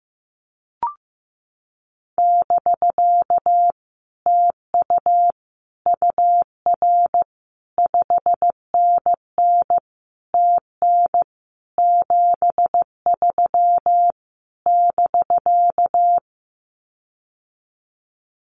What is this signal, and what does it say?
Morse code, heard as a keyed tone, about 15 wpm, 700 Hz. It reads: <BK> TU UR 5NN TN 73 <BK>
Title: Common QSO Phrases Encore 15wpm